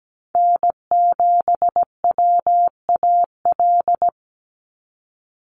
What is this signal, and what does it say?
Morse code, heard as a keyed tone, about 17 wpm, 700 Hz. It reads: N7WAL
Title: USA Call Signs 17wpm